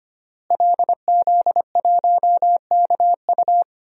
Morse code, heard as a keyed tone, about 25 wpm, 700 Hz. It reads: LZ1KU
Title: DXCC-100 Call Signs - 5 Characters - Repeated 3x - 25wpm